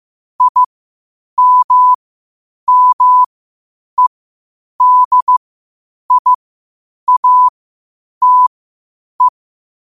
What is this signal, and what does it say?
Morse code, heard as a keyed tone, about 15 wpm, spaced as for 8 wpm, 1000 Hz. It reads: IMMEDIATE
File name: qotdfiles / 8wpm